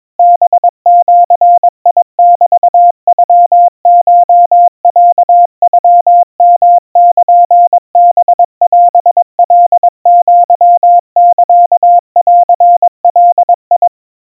Morse code (Japanese Wabun code), heard as a keyed tone, about 22 wpm, 700 Hz. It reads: ハジメノコロノヨルハオカアサンガ
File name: 80 cpm 700 Hz Crispy